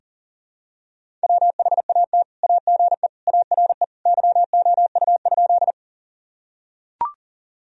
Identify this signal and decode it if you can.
Morse code, heard as a keyed tone, about 40 wpm, 700 Hz. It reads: WHAT AGE ARE YOU?
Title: Sentences from Top 1000 Words - Review Part 3 - 40wpm